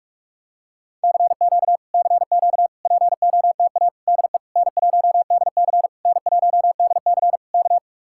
Morse code, heard as a keyed tone, about 49 wpm, 700 Hz. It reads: CQ CQ POTA DE N1DC N1DC K